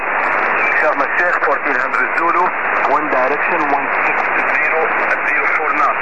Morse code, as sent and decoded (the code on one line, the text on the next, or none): none